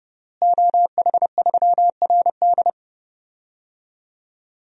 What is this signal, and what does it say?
Morse code, heard as a keyed tone, about 30 wpm, 700 Hz. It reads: OH3RD
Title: DXCC-100 Call Signs - 5 Characters - Repeated 3x - 30wpm